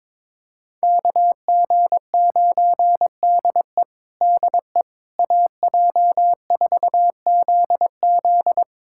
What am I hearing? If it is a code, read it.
Morse code, heard as a keyed tone, about 22 wpm, 700 Hz. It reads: KG9DE DE AJ4ZZ
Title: USA Callsign DE Callsign - 5 Characters - 22wpm